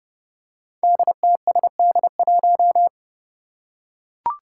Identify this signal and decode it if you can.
Morse code, heard as a keyed tone, about 30 wpm, 700 Hz. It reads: DTSD1
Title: Five Challenging Letter-Number Combinations 30wpm